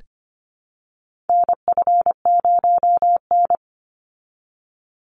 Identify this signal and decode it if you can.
Morse code, heard as a keyed tone, about 24 wpm, 700 Hz. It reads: NF0N
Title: Call Signs Repeated 3x - 25wpm